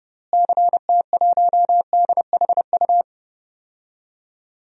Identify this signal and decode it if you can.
Morse code, heard as a keyed tone, about 30 wpm, 700 Hz. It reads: CT1DHU
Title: DXCC-100 Call Signs - 6 Characters - Repeated 3x - 30wpm